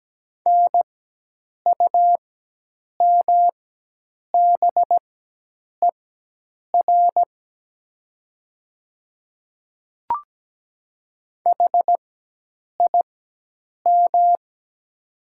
Morse code, heard as a keyed tone, about 17 wpm, spaced as for 8 wpm, 700 Hz. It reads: NUMBER HIM
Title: 4x Character Spacing - Top 200 Words - 17wpm